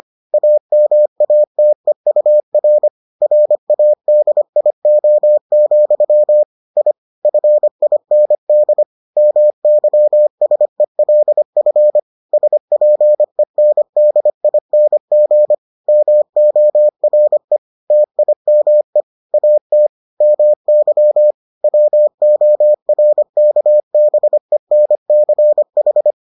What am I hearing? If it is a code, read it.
Morse code, heard as a keyed tone, about 25 wpm, 600 Hz. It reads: AMATEUR RADIO, I FIND MYSELF SPENDING MORE TIME AT MY WORKBENCH